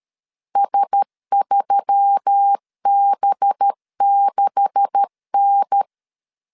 Morse code, heard as a keyed tone, about 13 wpm, 800 Hz. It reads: S3B6N
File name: CW / DET / 1 9 1500C S B 12wpm 3819